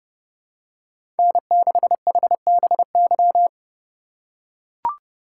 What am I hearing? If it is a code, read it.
Morse code, heard as a keyed tone, about 30 wpm, 700 Hz. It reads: N6HBY
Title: USA Call Signs - 2x Character Spacing - 30wpm